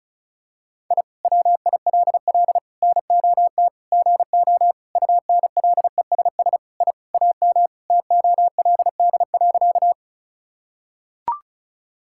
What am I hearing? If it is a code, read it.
Morse code, heard as a keyed tone, about 36 wpm, 700 Hz. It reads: I WILL NOT GO UNLESS I AM TOLD.
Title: Sentences from Top 1000 Words - Review Part 6 - 35wpm